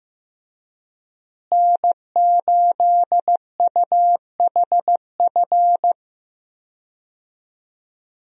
Morse code, heard as a keyed tone, about 15 wpm, 700 Hz. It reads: N8UHF
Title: USA Call Signs - 5 Characters - Repeated 2x - 15wpm